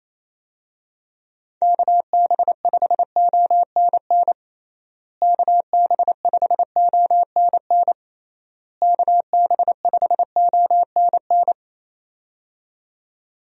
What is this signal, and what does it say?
Morse code, heard as a keyed tone, about 27 wpm, 700 Hz. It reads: KB5ONN KB5ONN KB5ONN
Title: USA Call Signs - 6 Characters - Repeated 3x - 28wpm